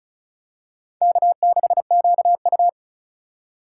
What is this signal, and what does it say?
Morse code, heard as a keyed tone, about 36 wpm, 700 Hz. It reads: K6QU
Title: USA Call Signs - 2x Character Spacing - 35wpm